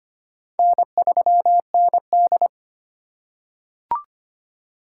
Morse code, heard as a keyed tone, about 25 wpm, 700 Hz. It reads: N3ND
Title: K1USN SST QSO Practice - 25wpm